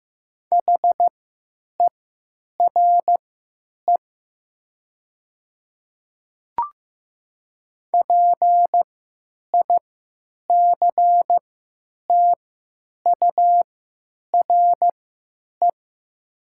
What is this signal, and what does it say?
Morse code, heard as a keyed tone, about 15 wpm, spaced as for 9 wpm, 700 Hz. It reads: HERE PICTURE
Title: 3x Character Spacing - Top 200 Words - 15wpm